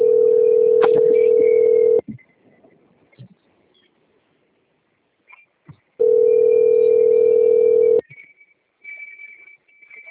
Ringback tone (North America), 440 + 480 Hz.